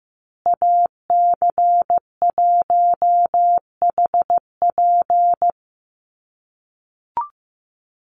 Morse code, heard as a keyed tone, about 15 wpm, 700 Hz. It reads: AC1HP
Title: USA Call Signs Repeated 2x 15wpm